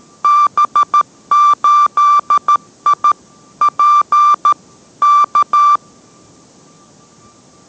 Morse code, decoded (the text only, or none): B8I P K